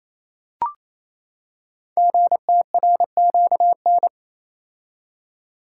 Morse code, heard as a keyed tone, about 27 wpm, 700 Hz. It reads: GTRQN
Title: Code Groups - 5 Letters - 28wpm